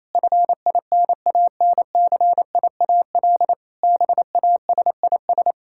Morse code, heard as a keyed tone, about 28 wpm, 700 Hz. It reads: FINANCIAL BASIS
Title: Sentences from Top 1000 Words - Review Part 5 - 28wpm